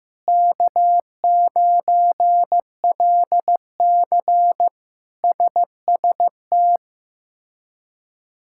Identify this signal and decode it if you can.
Morse code, heard as a keyed tone, about 15 wpm, 700 Hz. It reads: K9LC SST
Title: K1USN SST QSO Practice - 15wpm